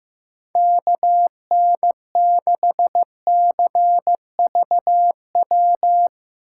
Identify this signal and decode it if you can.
Morse code, heard as a keyed tone, about 15 wpm, 700 Hz. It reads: KN6CVW
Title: USA Call Signs - 6 Characters - 8x Character Spacing - 15wpm